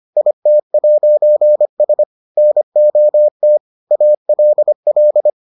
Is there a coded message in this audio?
Yes: Morse code from 0.2 to 5.3 s, heard as a keyed tone, about 25 wpm, 600 Hz. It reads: IT'S NOT ALL